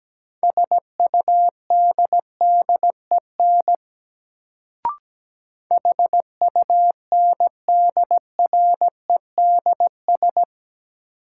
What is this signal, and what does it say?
Morse code, heard as a keyed tone, about 17 wpm, 700 Hz. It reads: SUDDEN HUNDREDS
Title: Sets of 4 Words - New Word Drill - 17wpm